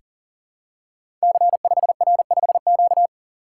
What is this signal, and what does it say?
Morse code, heard as a keyed tone, about 38 wpm, 700 Hz. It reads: C5R5Q
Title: Code Groups - 5 Letter:Numbers - 40wpm